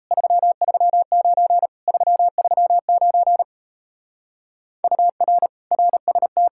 Morse code, heard as a keyed tone, about 38 wpm, 700 Hz. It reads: 339 339 UR RST